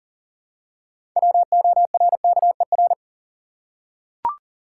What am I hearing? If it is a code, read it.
Morse code, heard as a keyed tone, about 40 wpm, 700 Hz. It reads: WORKER
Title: Six Letter Words - 40wpm